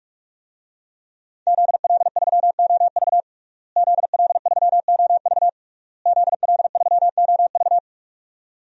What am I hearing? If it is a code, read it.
Morse code, heard as a keyed tone, about 46 wpm, 700 Hz. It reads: ZL3OV ZL3OV ZL3OV
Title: DXCC-100 Call Signs - 5 Characters - Repeated 3x - 45wpm